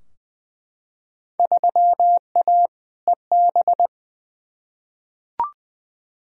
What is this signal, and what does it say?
Morse code, heard as a keyed tone, about 20 wpm, 700 Hz. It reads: 3A EB
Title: Field Day Exchanges Encore - 20wpm